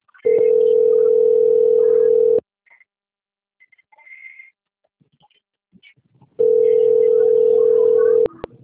Ringback tone (North America), 440 + 480 Hz.